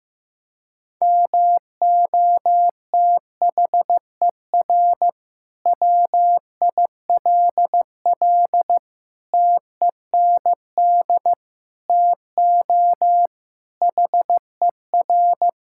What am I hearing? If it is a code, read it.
Morse code, heard as a keyed tone, about 15 wpm, 700 Hz. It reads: MOTHER WILL TEND TO HER
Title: Sentences from Top 1000 Words - Review Part 5 - 15wpm